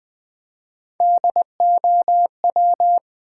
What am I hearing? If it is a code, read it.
Morse code, heard as a keyed tone, about 20 wpm, 700 Hz. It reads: DOW
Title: Build A Word - 1 Letter At A Time - 4-Letter Words - 20wpm